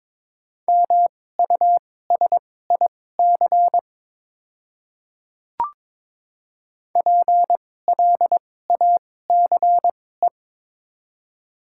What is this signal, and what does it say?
Morse code, heard as a keyed tone, about 22 wpm, spaced as for 16 wpm, 700 Hz. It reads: MUSIC PLACE
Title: 2x Character Spacing - Top 300 Words - 22wpm